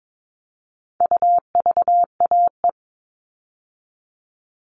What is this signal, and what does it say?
Morse code, heard as a keyed tone, about 22 wpm, 700 Hz. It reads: UVAE
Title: All Cut Numbers - 4-Digit 22wpm